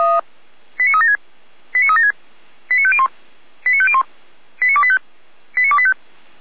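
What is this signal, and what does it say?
Five-tone selective call (ZVEI), heard as quick runs of five tones, tones of about 70 ms. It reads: 89376 89376 89501 89501 89285 89285
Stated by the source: ZVEI five-tone call 89280 received by SDR.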